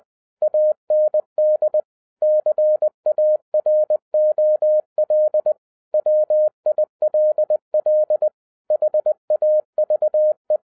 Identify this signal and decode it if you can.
Morse code, heard as a keyed tone, about 20 wpm, 610 Hz. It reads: AND CAROL WILL HAVE
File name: 2011-07-mp3 / n48 JOHN K3TN 20wpm